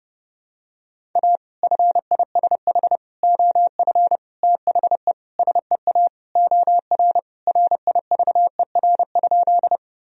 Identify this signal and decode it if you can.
Morse code, heard as a keyed tone, about 30 wpm, 700 Hz. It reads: A FISH OF THE SEA OR RIVER?
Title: Sentences from Top 1000 Words - Review Part 1 - 30wpm